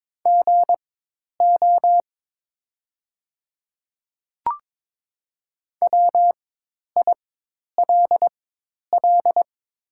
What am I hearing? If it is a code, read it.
Morse code, heard as a keyed tone, about 22 wpm, spaced as for 10 wpm, 700 Hz. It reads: GO WILL
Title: 4x Character Spacing - Top 100 Words - 22wpm